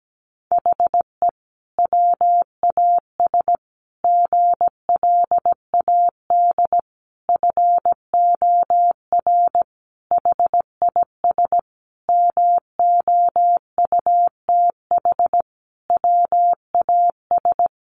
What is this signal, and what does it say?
Morse code, heard as a keyed tone, about 17 wpm, 700 Hz. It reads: HE WAS GLAD FOR HIS MOUTH WAS
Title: Sentences from Top 1500 Words 17wpm